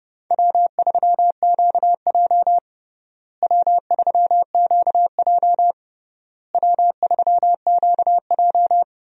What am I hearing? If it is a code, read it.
Morse code, heard as a keyed tone, about 32 wpm, 700 Hz. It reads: W3QJ W3QJ W3QJ
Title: USA Call Signs - 4 Characters - Repeated 3x - 30wpm